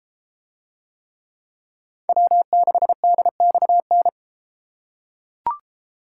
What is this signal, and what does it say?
Morse code, heard as a keyed tone, about 32 wpm, 700 Hz. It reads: W6DXN